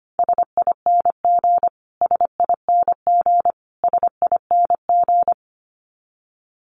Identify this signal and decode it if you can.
Morse code, heard as a keyed tone, about 25 wpm, 700 Hz. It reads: SING SING SING
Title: Top 500 Words Repeated 25wpm